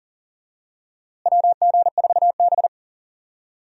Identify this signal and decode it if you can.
Morse code, heard as a keyed tone, about 40 wpm, 700 Hz. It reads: WG4B
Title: USA Call Signs - 4 Characters - Repeated 2x - 40wpm